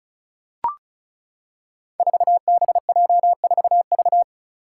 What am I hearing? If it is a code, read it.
Morse code, heard as a keyed tone, about 36 wpm, 700 Hz. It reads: 4BJ4V
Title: Five Challenging Letter-Number Combinations 35wpm